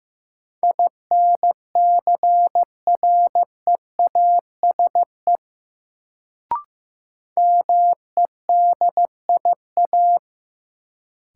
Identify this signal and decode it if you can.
Morse code, heard as a keyed tone, about 15 wpm, 700 Hz. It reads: INCREASE MEDIA
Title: Sentences from Top 1000 Words - Review Part 4 - New Word Drill - 15wpm